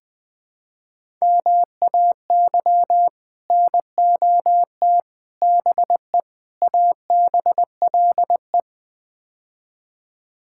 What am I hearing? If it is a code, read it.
Morse code, heard as a keyed tone, about 21 wpm, 700 Hz. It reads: MAY NOT BE ABLE
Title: Build A Phrase - 5-Word Phrases - 20wpm